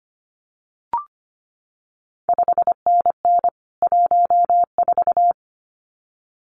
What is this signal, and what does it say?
Morse code, heard as a keyed tone, about 25 wpm, 700 Hz. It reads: 5NN 14
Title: CQ WW DX Contest Exchange Encore - 25wpm